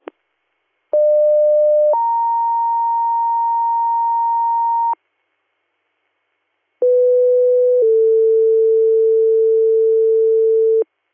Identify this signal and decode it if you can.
Two-tone page, heard as a one-second tone, then a three-second tone, about 600 then 930 Hz, about 500 then 450 Hz. It is heard from 0.9 to 10.8 s.